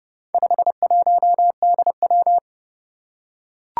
Morse code, heard as a keyed tone, about 30 wpm, 700 Hz. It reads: 51DW
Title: Four Challenging Letter-Number Combinations 30wpm